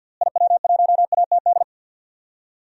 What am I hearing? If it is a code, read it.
Morse code, heard as a keyed tone, about 53 wpm, 700 Hz. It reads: IW1ATD
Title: DXCC-100 Call Signs - 6 Characters - Repeated 3x - 50wpm